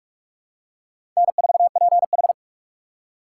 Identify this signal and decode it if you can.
Morse code, heard as a keyed tone, about 46 wpm, 700 Hz. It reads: N4PH
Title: USA Call Signs - 6x Character Spacing - 45wpm